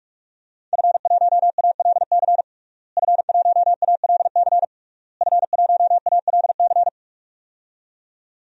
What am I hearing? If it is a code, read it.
Morse code, heard as a keyed tone, about 46 wpm, 700 Hz. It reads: F1ALC F1ALC F1ALC
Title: DXCC-100 Call Signs - 5 Characters - Repeated 3x - 45wpm